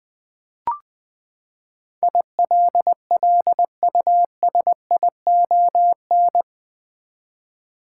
Morse code, heard as a keyed tone, about 21 wpm, 700 Hz. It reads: ILLUSION
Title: Eight Letter Words - 20wpm